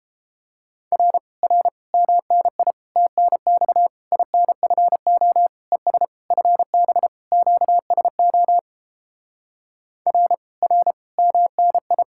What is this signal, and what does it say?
Morse code, heard as a keyed tone, about 29 wpm, 700 Hz. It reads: R R MNI TNX INFO ES FB QSO R R MNI